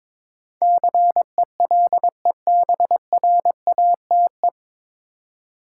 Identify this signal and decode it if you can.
Morse code, heard as a keyed tone, about 21 wpm, 700 Hz. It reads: CELEBRATE
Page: Morse Code Ninja practice 22wpm